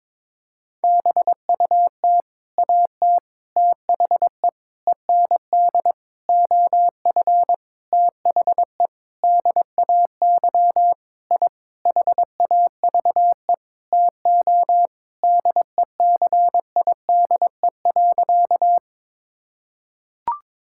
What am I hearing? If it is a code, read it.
Morse code, heard as a keyed tone, about 22 wpm, 700 Hz. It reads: BUT AT THE END OF THE DAY I HAVE TO DECIDE.
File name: practice / 22wpm / Sentences with 8-Word Phrases - 22wpm